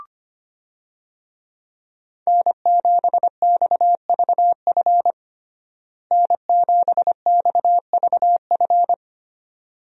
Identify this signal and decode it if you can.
Morse code, heard as a keyed tone, about 25 wpm, 700 Hz. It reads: N7XVF N7XVF
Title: USA Call Signs - 5 Characters - Repeated 2x - 25wpm